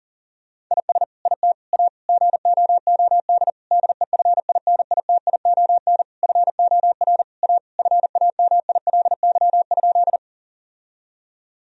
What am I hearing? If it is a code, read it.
Morse code, heard as a keyed tone, about 40 wpm, 700 Hz. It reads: IS IT A GOOD DEFINITION FOR A FAMILY?
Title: Sentences from Top 1000 Words - Review Part 4 - 40wpm